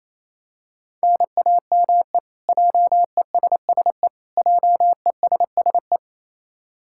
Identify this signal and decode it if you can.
Morse code, heard as a keyed tone, about 28 wpm, 700 Hz. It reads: NAME JESSE JESSE
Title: K1USN SST QSO Practice 28wpm